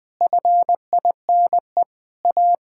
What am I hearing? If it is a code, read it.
Morse code, heard as a keyed tone, about 20 wpm, 700 Hz. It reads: FINE A
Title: Sets of 9 Words - 20wpm